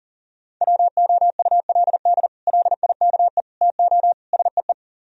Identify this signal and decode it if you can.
Morse code, heard as a keyed tone, about 52 wpm, 700 Hz. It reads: WOULD LIKE TO SEE